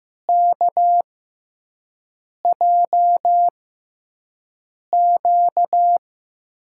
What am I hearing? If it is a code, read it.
Morse code, heard as a keyed tone, about 15 wpm, spaced as for 5 wpm, 700 Hz. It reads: KJQ
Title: USA Call Signs - 5 Characters - 6x Character Spacing - 15wpm